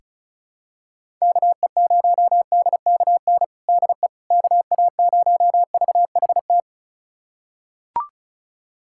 Morse code, heard as a keyed tone, about 36 wpm, 700 Hz. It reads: KE0DKN DE KA0VHT
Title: USA Callsign DE Callsign - 6 Characters - 35wpm